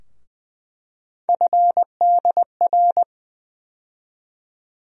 Morse code, heard as a keyed tone, about 20 wpm, 700 Hz. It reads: FDR
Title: USA Call Sign Postfixes - 3 Letters 20wpm